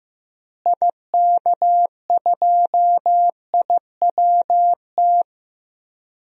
Morse code, heard as a keyed tone, about 15 wpm, 700 Hz. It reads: IK2IWT
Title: DXCC-100 Call Signs - 6 Characters - Repeated 3x - 15wpm